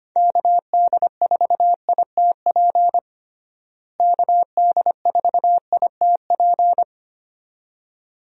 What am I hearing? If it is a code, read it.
Morse code, heard as a keyed tone, about 25 wpm, 700 Hz. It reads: KD4ITP KD4ITP
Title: USA Call Signs - 6 Characters - Repeated 2x - 25wpm